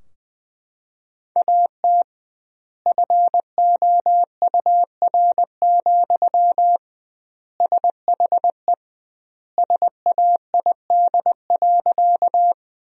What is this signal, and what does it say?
Morse code, heard as a keyed tone, about 21 wpm, 700 Hz. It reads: AT FOUR, SHE SAID.
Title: 2x Word-Spacing - Sentences from Top 300 Words - 20wpm